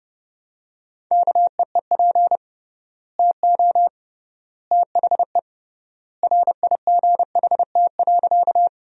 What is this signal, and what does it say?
Morse code, heard as a keyed tone, about 29 wpm, 700 Hz. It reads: KEEP TO THE RIGHT.
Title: 3x Word-Spacing - Sentences from Top 300 Words - 30wpm